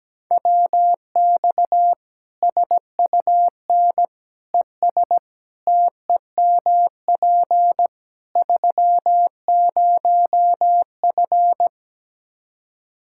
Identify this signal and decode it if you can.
Morse code, heard as a keyed tone, about 17 wpm, 700 Hz. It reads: WX SUN ES TEMP 30F
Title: Practice QSO - Exchange 2 - Info - 17wpm